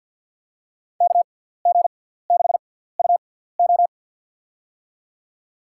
Morse code, heard as a keyed tone, about 50 wpm, spaced as for 17 wpm, 700 Hz. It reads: KG6UO
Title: USA Call Signs - 5 Characters - 6x Character Spacing - 50wpm